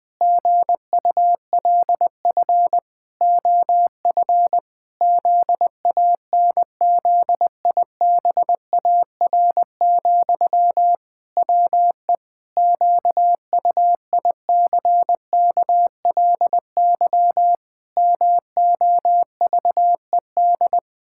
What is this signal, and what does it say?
Morse code, heard as a keyed tone, about 20 wpm, 700 Hz. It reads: GULF OF ZANZIBAR, WE QUICKLY MOVED
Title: Pangram Challenge v2 - 20wpm